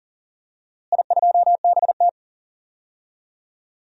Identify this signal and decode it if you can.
Morse code, heard as a keyed tone, about 40 wpm, 700 Hz. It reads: I2BT
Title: DXCC-100 Call Signs - 4 Characters - Repeated 2x - 40wpm